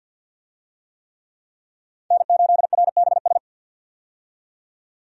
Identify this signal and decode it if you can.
Morse code, heard as a keyed tone, about 51 wpm, 700 Hz. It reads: N8RBS
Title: USA Call Signs - 5 Characters - 50wpm